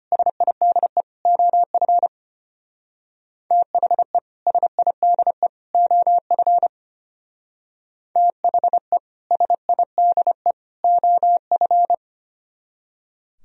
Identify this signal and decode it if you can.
Morse code, heard as a keyed tone, about 32 wpm, 700 Hz. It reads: SIDE OF THE SIDE OF THE SIDE OF